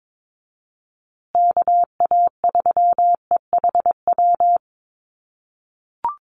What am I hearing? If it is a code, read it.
Morse code, heard as a keyed tone, about 22 wpm, 700 Hz. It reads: KA3EHW
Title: USA Call Signs Repeated 3x 22wpm